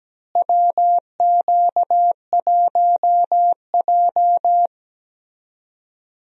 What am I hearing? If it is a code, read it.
Morse code, heard as a keyed tone, about 17 wpm, 700 Hz. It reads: WQ1J
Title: USA Call Signs - 4 Characters - Repeated 3x - 17wpm